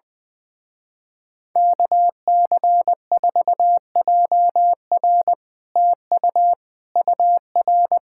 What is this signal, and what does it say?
Morse code, heard as a keyed tone, about 20 wpm, 700 Hz. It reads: KC4JR TU UR
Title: Speed-Racing Common QSO Phrases - 20wpm